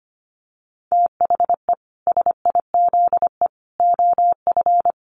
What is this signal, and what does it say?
Morse code, heard as a keyed tone, about 25 wpm, 700 Hz. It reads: THE SIZE OF
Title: Dit Challenge 25wpm